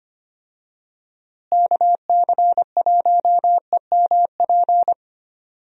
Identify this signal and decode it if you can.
Morse code, heard as a keyed tone, about 25 wpm, 700 Hz. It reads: KC1EMP
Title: USA Call Signs - 6 Characters - Repeated 2x - 25wpm